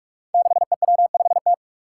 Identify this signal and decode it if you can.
Morse code, heard as a keyed tone, about 46 wpm, 700 Hz. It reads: 6EW5T
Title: Five Challenging Letter-Number Combinations 45wpm